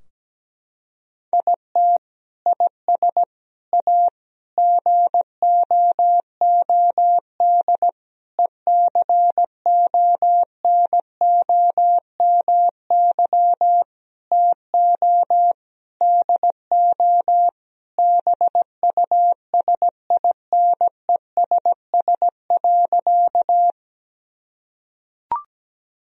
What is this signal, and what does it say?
Morse code, heard as a keyed tone, about 17 wpm, 700 Hz. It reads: IT IS A GOOD ECONOMY TO DO BUSINESS.
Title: Sentences from Top 1000 Words - Review Part 3 - 17wpm